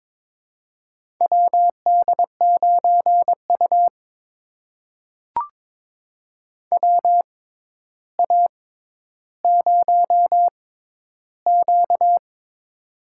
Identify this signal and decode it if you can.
Morse code, heard as a keyed tone, about 22 wpm, spaced as for 11 wpm, 700 Hz. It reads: WD9U WA0Q
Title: USA Call Signs - 4 Characters - 6x Character Spacing - 22wpm